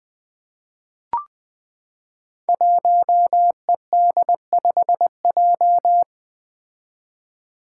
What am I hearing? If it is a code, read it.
Morse code, heard as a keyed tone, about 20 wpm, 700 Hz. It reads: 1ED5J
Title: Five Challenging Letter-Number Combinations 20wpm